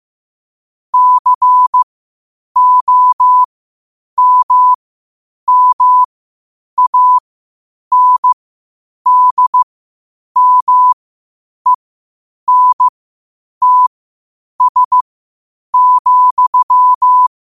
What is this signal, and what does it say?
Morse code, heard as a keyed tone, about 15 wpm, spaced as for 8 wpm, 1000 Hz. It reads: COMMANDMENTS,
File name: qotdfiles / 8wpm